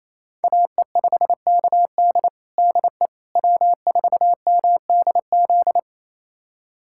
Morse code, heard as a keyed tone, about 28 wpm, 700 Hz. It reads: AE5KD DE W4MDZ
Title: USA Callsign DE Callsign - 5 Characters - 28wpm